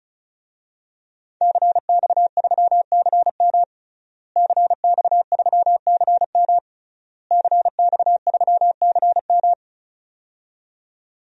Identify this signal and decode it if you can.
Morse code, heard as a keyed tone, about 36 wpm, 700 Hz. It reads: CX3CM CX3CM CX3CM
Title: DXCC-100 Call Signs - 5 Characters - Repeated 3x - 35wpm